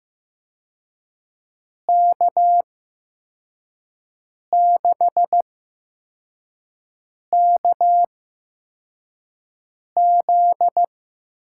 Morse code, heard as a keyed tone, about 16 wpm, spaced as for 4 wpm, 700 Hz. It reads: K6KZ
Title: USA Call Signs - 6 Characters - 8x Character Spacing - 15wpm